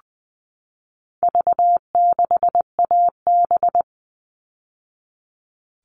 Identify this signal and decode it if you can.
Morse code, heard as a keyed tone, about 20 wpm, 700 Hz. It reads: V6AB
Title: All Cut Numbers - 4-Digit 20wpm